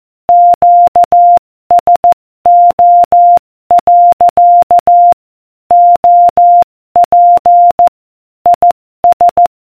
Morse code, heard as a keyed tone, about 14 wpm, 700 Hz. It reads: QSO. OP IS